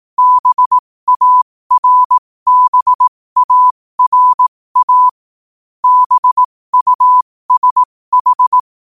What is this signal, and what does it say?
Morse code, heard as a keyed tone, about 18 wpm, spaced as for 15 wpm, 1000 Hz. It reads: BARBARA BUSH
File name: qotdfiles / 15wpm